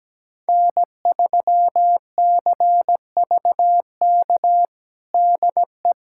Morse code, heard as a keyed tone, about 17 wpm, 700 Hz. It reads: N3CVK DE
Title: USA Callsign DE Callsign - 5 Characters - 17wpm